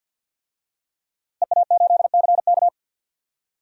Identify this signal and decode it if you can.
Morse code, heard as a keyed tone, about 51 wpm, 700 Hz. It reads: EA8CK
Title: DXCC-100 Call Signs - 5 Characters - Repeated 2x - 50wpm